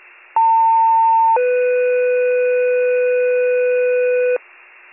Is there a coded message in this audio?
Yes: two-tone page from 0.4 to 4.4 s, heard as a one-second tone, then a three-second tone, about 900 then 500 Hz, with noise.